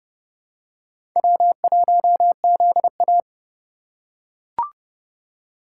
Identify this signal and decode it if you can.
Morse code, heard as a keyed tone, about 31 wpm, 700 Hz. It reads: W1ZA
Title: USA Call Signs - 3x Character Spacing - 30wpm